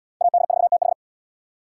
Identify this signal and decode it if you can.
Morse code, heard as a keyed tone, about 74 wpm, 700 Hz. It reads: AD4TH